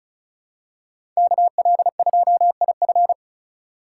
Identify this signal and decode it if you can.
Morse code, heard as a keyed tone, about 36 wpm, 700 Hz. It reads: KL2IF
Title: USA Call Signs - 5 Characters - 2x Character Spacing - 35wpm